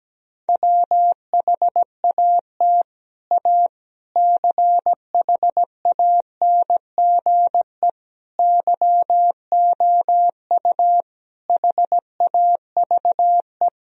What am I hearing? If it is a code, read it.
Morse code, heard as a keyed tone, about 17 wpm, 700 Hz. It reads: WHAT A CHANGE YOU HAVE
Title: Sentences from Top 1000 Words - Review Part 1 - 17wpm